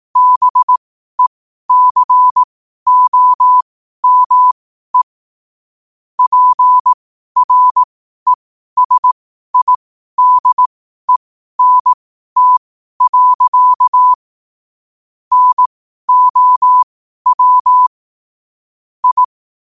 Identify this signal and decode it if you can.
Morse code, heard as a keyed tone, about 18 wpm, spaced as for 12 wpm, 1000 Hz. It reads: BECOME PRESIDENT. NOW I